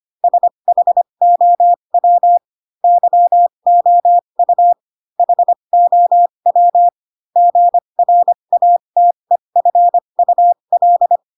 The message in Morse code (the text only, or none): SHOW YOU HOW GRATEFUL